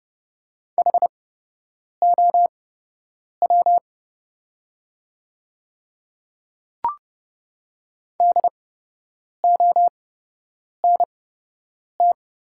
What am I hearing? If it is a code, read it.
Morse code, heard as a keyed tone, about 30 wpm, spaced as for 8 wpm, 690 Hz. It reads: HOW DONT
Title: 8x Character Spacing - Top 300 Words - 30wpm